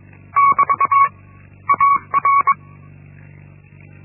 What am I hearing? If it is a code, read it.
Morse code, heard as a keyed tone, about 22 wpm, 1120 Hz. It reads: <BT> AR
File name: Linksmakalnis-1